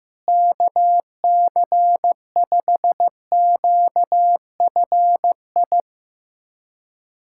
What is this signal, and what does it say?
Morse code, heard as a keyed tone, about 16 wpm, 700 Hz. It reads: KC5QFI
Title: USA Callsign DE Callsign - 6 Characters - 15wpm